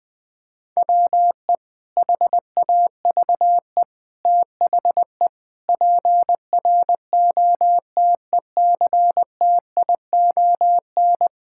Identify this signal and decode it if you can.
Morse code, heard as a keyed tone, about 21 wpm, 700 Hz. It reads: WE HAVE THE PROTECTION
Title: Sentences from Top 1600 Words 20wpm